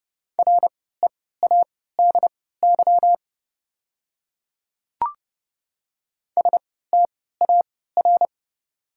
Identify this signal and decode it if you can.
Morse code, heard as a keyed tone, about 31 wpm, spaced as for 17 wpm, 690 Hz. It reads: READY STAR
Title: 3x Character Spacing - Top 500 Words - 30wpm